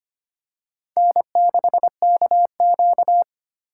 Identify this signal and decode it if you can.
Morse code, heard as a keyed tone, about 25 wpm, 700 Hz. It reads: N6KQ
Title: USA Call Signs - 4 Characters - 4x Character Spacing - 25wpm